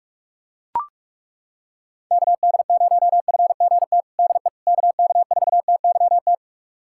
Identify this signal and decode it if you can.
Morse code, heard as a keyed tone, about 41 wpm, 700 Hz. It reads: KD0FGT DE KK4TYT
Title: USA Callsign DE Callsign - 6 Characters - 45wpm